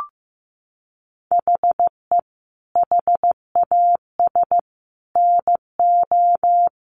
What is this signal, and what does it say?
Morse code, heard as a keyed tone, about 15 wpm, 700 Hz. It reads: HE HAS NO
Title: Sentences from Top 1200 Words 15wpm